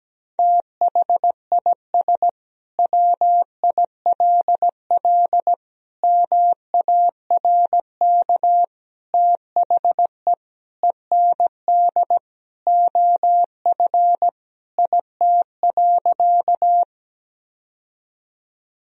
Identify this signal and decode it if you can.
Morse code, heard as a keyed tone, about 17 wpm, 700 Hz. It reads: THIS WILL MARK THE END OF IT.
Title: Sentences from Top 1000 Words - Review Part 1 - 17wpm